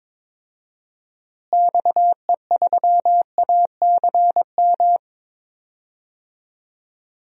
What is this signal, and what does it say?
Morse code, heard as a keyed tone, about 22 wpm, 700 Hz. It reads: XE3ACM
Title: DXCC-100 Call Signs - 6 Characters - Repeated 3x - 22wpm